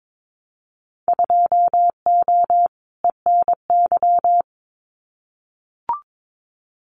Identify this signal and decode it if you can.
Morse code, heard as a keyed tone, about 22 wpm, 700 Hz. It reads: 2O ENY
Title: Winter Field Day Exchanges 22wpm